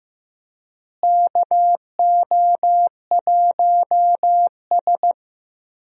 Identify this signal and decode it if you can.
Morse code, heard as a keyed tone, about 15 wpm, 700 Hz. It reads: KO1S
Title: USA Call Signs - 4 Characters - Repeated 3x - 15wpm